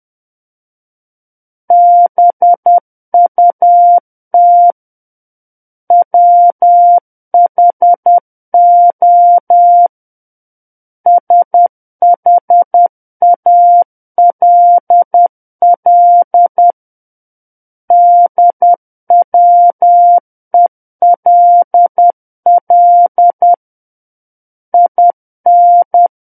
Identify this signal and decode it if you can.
Morse code, heard as a keyed tone, about 10 wpm, 700 Hz. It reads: BUT WHO SHALL DWELL IN
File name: Book One / 01-Chapter 1 - 10 WPM